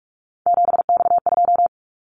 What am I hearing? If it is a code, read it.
Morse code, heard as a keyed tone, about 46 wpm, 700 Hz. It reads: 7X2
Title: Callsign Prefixes for Top 100 DXCC - 45wpm